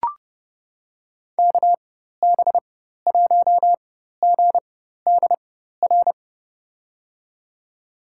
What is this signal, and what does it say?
Morse code, heard as a keyed tone, about 30 wpm, spaced as for 14 wpm, 700 Hz. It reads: KB1GDR
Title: USA Call Signs - 6 Characters - 4x Character Spacing - 30wpm